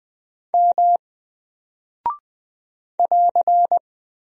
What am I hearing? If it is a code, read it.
Morse code, heard as a keyed tone, about 20 wpm, 700 Hz. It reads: M <AR>
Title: Single Character (Letter:Number:Punctuation:Prosign) 20wpm